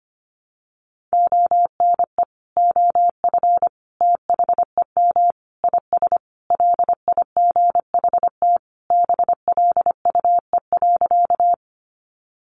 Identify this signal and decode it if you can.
Morse code, heard as a keyed tone, about 25 wpm, 700 Hz. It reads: ONE OF THEM IS LIGHT BLUE.
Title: Additional Easy Phrases 25wpm